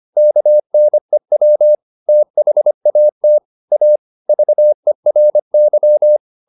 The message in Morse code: KNEW THAT A VERY

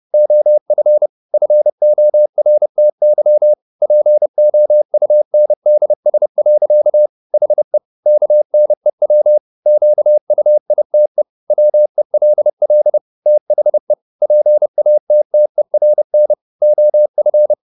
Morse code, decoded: OF FORTY POUNDS. HE KNEW QUITE WELL THE PATTERN OF

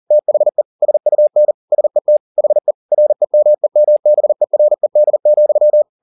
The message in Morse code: THE SUN SET HE REMEMBERED,

